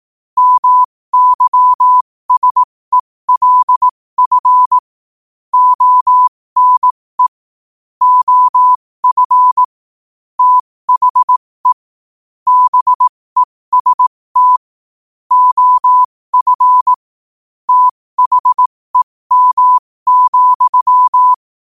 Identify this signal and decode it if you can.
Morse code, heard as a keyed tone, about 18 wpm, spaced as for 15 wpm, 1000 Hz. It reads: MYSELF ONE OF THE BEST OF THEM,
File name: qotdfiles / 15wpm